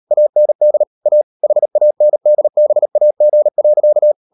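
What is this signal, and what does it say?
Morse code, heard as a keyed tone, about 38 wpm, 600 Hz. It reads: AND A HANDBAG.
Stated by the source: Morse code practice at 38 WPM, sending Ken Follett's novel Never in English, from CW med Gnister.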